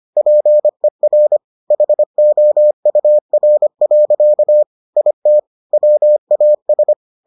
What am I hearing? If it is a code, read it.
Morse code, heard as a keyed tone, about 25 wpm, 600 Hz. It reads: PER HOUR. IT WAS